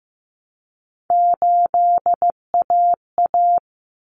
Morse code, heard as a keyed tone, about 15 wpm, 700 Hz. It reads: 8AA
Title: Common Cut Numbers - 3-Digit 15wpm